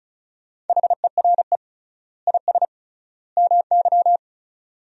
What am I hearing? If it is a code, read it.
Morse code, heard as a keyed tone, about 36 wpm, 700 Hz. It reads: HERE IS MY
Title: 3x Word-Spacing - Sentences from Top 200 Words - 35wpm